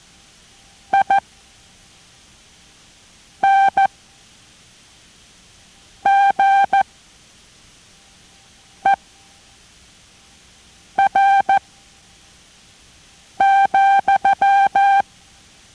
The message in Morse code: INGER,